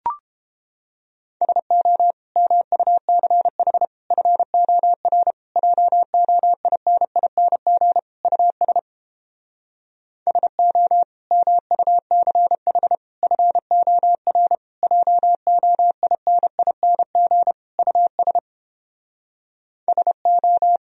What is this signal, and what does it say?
Morse code, heard as a keyed tone, about 29 wpm, 700 Hz. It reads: SO MUCH FOR JOINING US SO MUCH FOR JOINING US SO